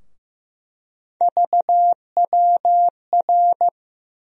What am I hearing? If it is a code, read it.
Morse code, heard as a keyed tone, about 15 wpm, 700 Hz. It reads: VWR
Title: USA Call Sign Postfixes - 3 Letters 15wpm